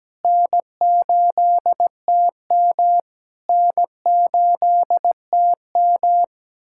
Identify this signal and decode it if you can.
Morse code, heard as a keyed tone, about 17 wpm, 700 Hz. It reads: N8TM N8TM
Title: SOTA QSO Practice 17wpm